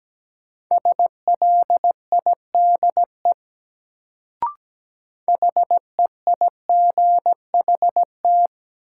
Morse code, heard as a keyed tone, about 17 wpm, 700 Hz. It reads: SLIDE HEIGHT
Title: Sentences from Top 1600 Words - New Word Drill Alt - 17wpm